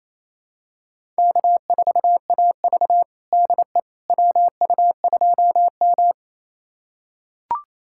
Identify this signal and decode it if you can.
Morse code, heard as a keyed tone, about 28 wpm, 700 Hz. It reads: K4AV DE WU2M
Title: USA Callsign DE Callsign - 4 Characters - 28wpm